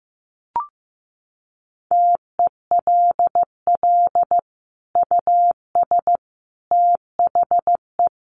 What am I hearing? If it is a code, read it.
Morse code, heard as a keyed tone, about 15 wpm, 700 Hz. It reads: TELL US THE
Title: Sentences from Top 1400 Words 15wpm